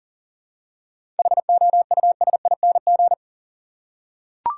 Morse code, heard as a keyed tone, about 40 wpm, 700 Hz. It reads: HOUSING
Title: Seven Letter Words - 40wpm